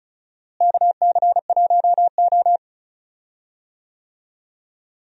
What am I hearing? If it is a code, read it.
Morse code, heard as a keyed tone, about 34 wpm, 700 Hz. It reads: KC1O
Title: USA Call Signs - 4 Characters - Repeated 2x - 35wpm